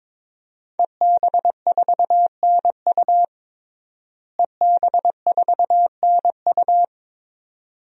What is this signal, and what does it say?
Morse code, heard as a keyed tone, about 21 wpm, 700 Hz. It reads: EB4NU EB4NU
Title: DXCC-100 Call Signs - 5 Characters - Repeated 2x - 22wpm